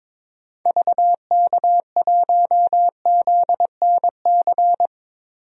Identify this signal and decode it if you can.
Morse code, heard as a keyed tone, about 22 wpm, 700 Hz. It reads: VK1ZNC
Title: DXCC-100 Call Signs - 6 Characters - 22wpm